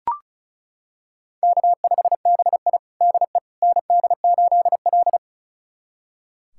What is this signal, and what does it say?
Morse code, heard as a keyed tone, about 36 wpm, 700 Hz. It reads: K5BI DE ND8L
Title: USA Callsign DE Callsign - 4 Characters - 35wpm